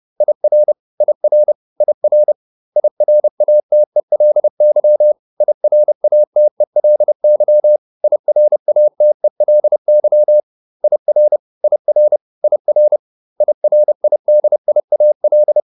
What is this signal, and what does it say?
Morse code, heard as a keyed tone, about 30 wpm, 600 Hz. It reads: IR IR IR IRATELY IRATELY IRATELY IR IR IR IRIDIAL